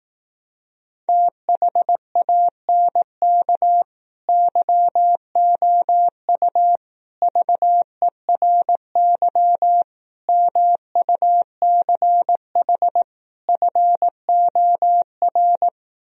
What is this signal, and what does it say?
Morse code, heard as a keyed tone, about 18 wpm, 700 Hz. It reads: THANK YOU VERY MUCH FOR
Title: Speed-Racing - Sets of 5 Words 15wpm